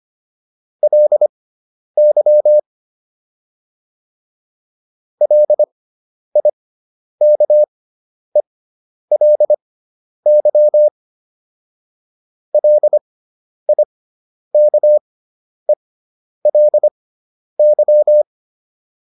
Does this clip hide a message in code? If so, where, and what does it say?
Yes: Morse code from 0.8 to 18.2 s, heard as a keyed tone, about 25 wpm, spaced as for 10 wpm, 600 Hz. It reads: LY LIKELY LIKELY